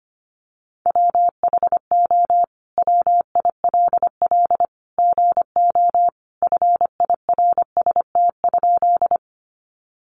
Morse code, heard as a keyed tone, about 25 wpm, 700 Hz. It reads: WHO WILL GO FIRST?
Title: Sentences from Top 100 Words 25wpm